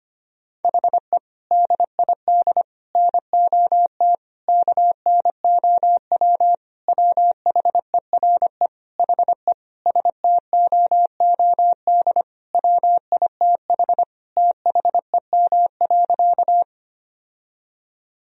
Morse code, heard as a keyed tone, about 25 wpm, 700 Hz. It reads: HE DID NOT KNOW WHERE HE STOOD WITH THEM.
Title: Sentences from Top 1000 Words - Review Part 2 - 25wpm